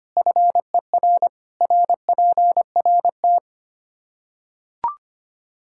Morse code, heard as a keyed tone, about 25 wpm, 700 Hz. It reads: FER RPRT
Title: Practice QSO - Exchange 2 - Info - 25wpm